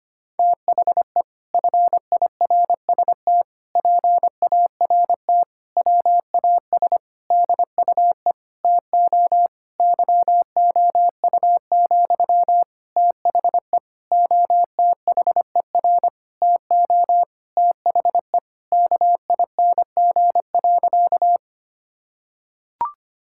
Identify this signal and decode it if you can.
Morse code, heard as a keyed tone, about 25 wpm, 700 Hz. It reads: THE FIRST PART WAS DUE TO YOU, THE OTHER TO THE KING.
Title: Sentences from Top 1000 Words - Review Part 6 - 25wpm